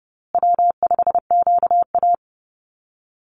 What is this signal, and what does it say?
Morse code, heard as a keyed tone, about 30 wpm, 700 Hz. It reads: W5QA
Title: USA Call Signs Repeated 2x 30wpm